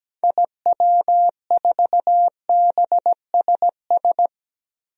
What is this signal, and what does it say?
Morse code, heard as a keyed tone, about 17 wpm, 700 Hz. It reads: IW4BSS
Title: DXCC-100 Call Signs - 6 Characters - Repeated 3x - 17wpm